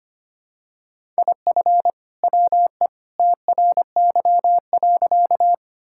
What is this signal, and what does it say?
Morse code, heard as a keyed tone, about 25 wpm, 700 Hz. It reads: IF WE TRY.
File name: practice / 25wpm / Sentences with 2-Word Phrases - 25wpm